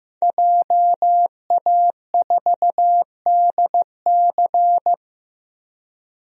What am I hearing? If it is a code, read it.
Morse code, heard as a keyed tone, about 15 wpm, 700 Hz. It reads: JA4DC
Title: DXCC-100 Call Signs - 5 Characters - Repeated 2x - 15wpm